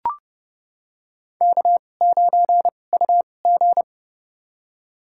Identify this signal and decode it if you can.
Morse code, heard as a keyed tone, about 30 wpm, spaced as for 22 wpm, 700 Hz. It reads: K9UG
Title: USA Call Signs - 4 Characters - 2x Character Spacing - 30wpm